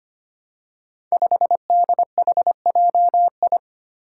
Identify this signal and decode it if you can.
Morse code, heard as a keyed tone, about 25 wpm, 700 Hz. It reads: 5DHJI
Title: Five Challenging Letter-Number Combinations 25wpm